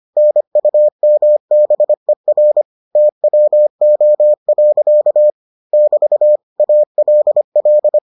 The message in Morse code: NUMBER TWO. <BT> ALL

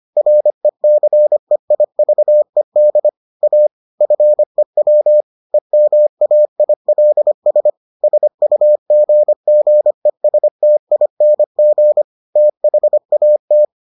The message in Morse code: RECEIVED A FEW EMAILS SUGGESTING THAT

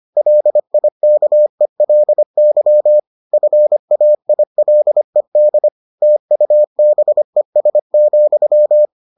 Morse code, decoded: LIKELY FAILED TUBES,

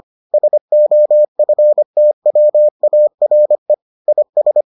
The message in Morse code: SOFTWARE IS

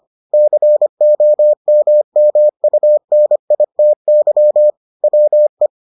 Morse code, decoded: COMMUNITY WE

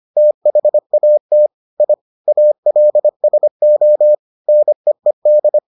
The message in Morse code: THAT I ALSO NEED

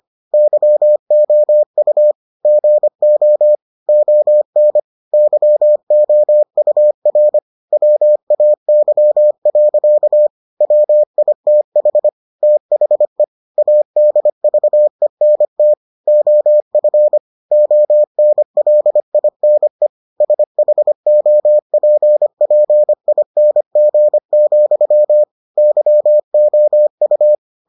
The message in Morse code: YOU GO ON YOUR WAY. WITH THE ADVENT OF ONLINE SHOPPING, YOU

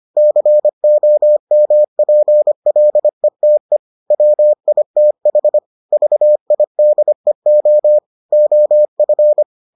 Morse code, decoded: COMPLETE WITH VIDEO OF

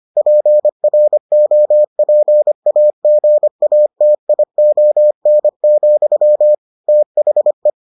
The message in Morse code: PROPAGATION, THE

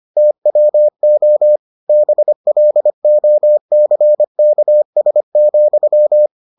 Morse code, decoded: TWO BLOCKS,

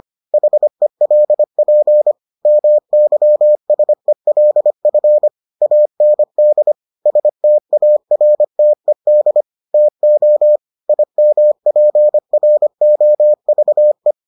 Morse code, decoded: HELP MYSELF AND STARTED TO IMPROVE

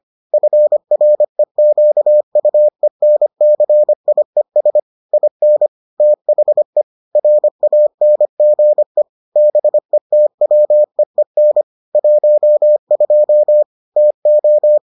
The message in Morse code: FREQUENCIES IN THE RANGE BETWEEN 12 TO

